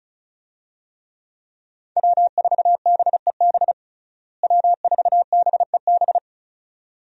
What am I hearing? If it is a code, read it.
Morse code, heard as a keyed tone, about 36 wpm, 700 Hz. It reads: W4BEB W4BEB
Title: USA Call Signs - 5 Characters - Repeated 2x - 35wpm